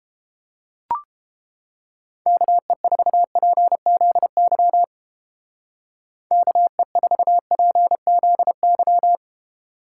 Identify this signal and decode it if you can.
Morse code, heard as a keyed tone, about 30 wpm, 700 Hz. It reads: KE4PZY KE4PZY